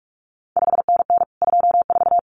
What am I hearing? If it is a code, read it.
Morse code, heard as a keyed tone, about 46 wpm, 700 Hz. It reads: 5NN 24
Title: CQ WW DX Contest Exchange Encore - 45wpm